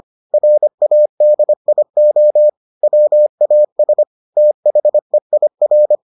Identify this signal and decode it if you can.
Morse code, heard as a keyed tone, about 25 wpm, 600 Hz. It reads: RADIO WAS THEIR